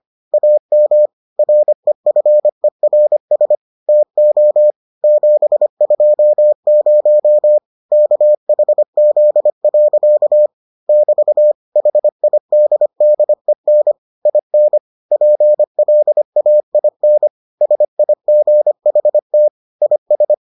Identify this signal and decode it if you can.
Morse code, heard as a keyed tone, about 25 wpm, 600 Hz. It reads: AM REFERS TO 720 KHZ. <BT> HIDDEN IN PLAIN SIGHT IS